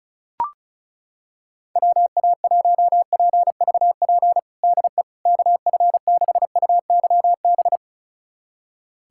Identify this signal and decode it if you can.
Morse code, heard as a keyed tone, about 34 wpm, 690 Hz. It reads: WA1PVP DE KF6UYB
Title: USA Callsign DE Callsign - 6 Characters - 35wpm